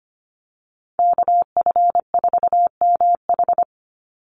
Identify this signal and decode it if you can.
Morse code, heard as a keyed tone, about 25 wpm, 700 Hz. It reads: KF4MH
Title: NA SOTA Chaser Callsigns 25wpm